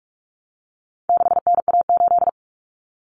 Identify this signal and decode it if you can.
Morse code, heard as a keyed tone, about 44 wpm, 700 Hz. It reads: 6NA8
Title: Common Cut Numbers - 4-Digit 45wpm